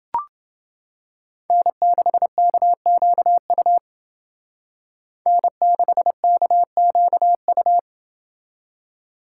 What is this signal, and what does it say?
Morse code, heard as a keyed tone, about 28 wpm, 700 Hz. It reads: N6KQU N6KQU